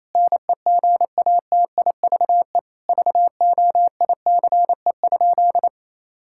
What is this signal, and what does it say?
Morse code, heard as a keyed tone, about 27 wpm, 700 Hz. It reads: NEGATIVE VOICE?
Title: Sentences from Top 1000 Words - Review Part 6 - 28wpm